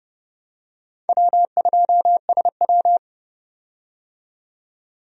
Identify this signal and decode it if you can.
Morse code, heard as a keyed tone, about 30 wpm, 700 Hz. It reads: W2SW
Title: USA Call Signs - 4 Characters - 30wpm